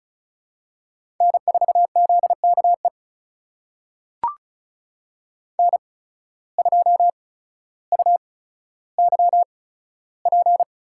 Morse code, heard as a keyed tone, about 36 wpm, spaced as for 15 wpm, 700 Hz. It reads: N4ZKE N2UYP
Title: USA Call Signs - 5 Characters - 8x Character Spacing - 35wpm